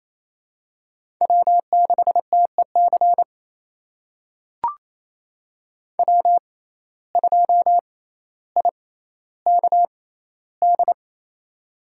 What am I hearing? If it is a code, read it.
Morse code, heard as a keyed tone, about 28 wpm, spaced as for 14 wpm, 700 Hz. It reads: W6TEC W2IKD
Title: USA Call Signs - 5 Characters - 6x Character Spacing - 28wpm